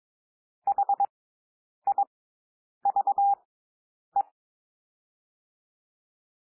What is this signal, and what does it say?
Morse code, heard as a keyed tone, about 22 wpm, spaced as for 9 wpm, 800 Hz. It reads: HIVE
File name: SOUND WORDS 22 wpm